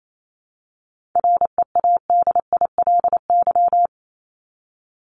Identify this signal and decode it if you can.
Morse code, heard as a keyed tone, about 28 wpm, 700 Hz. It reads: READILY
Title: LY Suffix - Top 100 Words with Base Words 28wpm